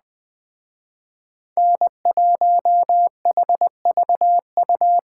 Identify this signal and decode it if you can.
Morse code, heard as a keyed tone, about 20 wpm, 700 Hz. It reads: N1HVU